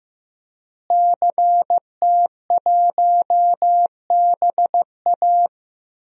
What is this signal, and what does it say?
Morse code, heard as a keyed tone, about 15 wpm, 700 Hz. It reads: CT1BA
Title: DXCC-100 Call Signs - 5 Characters - Repeated 3x - 15wpm